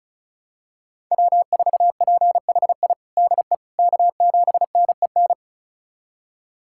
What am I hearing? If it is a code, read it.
Morse code, heard as a keyed tone, about 36 wpm, 700 Hz. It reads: W4PHI DE K7NEN
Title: USA Callsign DE Callsign - 5 Characters - 35wpm